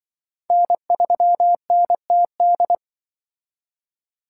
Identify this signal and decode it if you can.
Morse code, heard as a keyed tone, about 24 wpm, 700 Hz. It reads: N3NTD
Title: Speed Racing - USA 1x3 Call Signs 20wpm